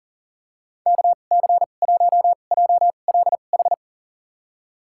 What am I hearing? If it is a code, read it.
Morse code, heard as a keyed tone, about 40 wpm, spaced as for 29 wpm, 700 Hz. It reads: KC1JLH
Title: USA Call Signs - 6 Characters - 2x Character Spacing - 40wpm